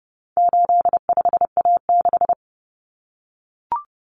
Morse code, heard as a keyed tone, about 30 wpm, 700 Hz. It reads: 85A6
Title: Common Cut Numbers - 4-Digit 30wpm